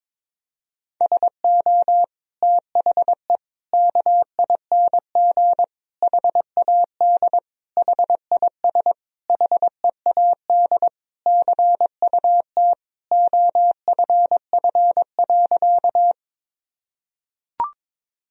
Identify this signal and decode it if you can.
Morse code, heard as a keyed tone, about 22 wpm, 700 Hz. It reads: SO THE KING HAD HIS HEAD CUT OFF.
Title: Sentences from Top 1000 Words - Review Part 1 - 22wpm